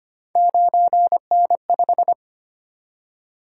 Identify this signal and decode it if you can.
Morse code, heard as a keyed tone, about 25 wpm, 700 Hz. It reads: 9N5
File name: Three Letter-Number Combinations - 25wpm